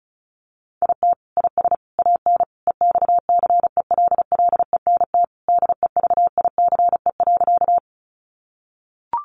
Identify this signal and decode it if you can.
Morse code, heard as a keyed tone, about 34 wpm, 700 Hz. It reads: IT IS AN EXCELLENT DEVICE.
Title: Sentences from Top 1500 Words 35wpm